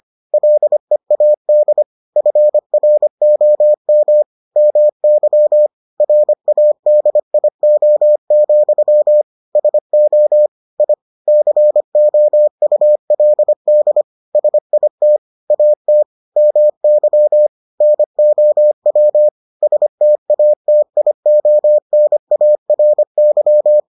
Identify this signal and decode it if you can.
Morse code, heard as a keyed tone, about 25 wpm, 600 Hz. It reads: LEAD FROM MY RADIO, SO I COULD SIT AT MY NOW STATIONARY